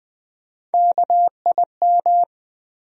Morse code, heard as a keyed tone, about 20 wpm, 700 Hz. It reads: KIM
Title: USA Common Names - Max 3 Letters - 20wpm